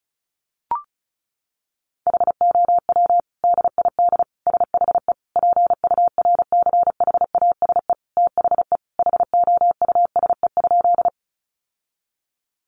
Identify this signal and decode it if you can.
Morse code, heard as a keyed tone, about 34 wpm, 700 Hz. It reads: HOW DID SHE PURCHASE THE HOUSE?
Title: Sentences from Top 800 Words 35wpm